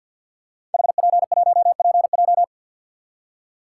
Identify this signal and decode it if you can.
Morse code, heard as a keyed tone, about 45 wpm, 700 Hz. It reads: HP1PJ
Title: Code Groups - 5 Letter:Numbers - 50wpm